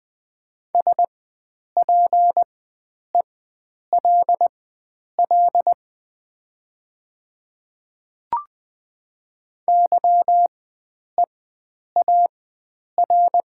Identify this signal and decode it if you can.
Morse code, heard as a keyed tone, about 20 wpm, spaced as for 9 wpm, 700 Hz. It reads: SPELL YEAR
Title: 4x Character Spacing - Top 300 Words - 20wpm